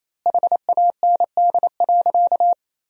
Morse code, heard as a keyed tone, about 28 wpm, 700 Hz. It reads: HAND.
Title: 3x Word-Spacing - Sentences from Top 400 Words - 28wpm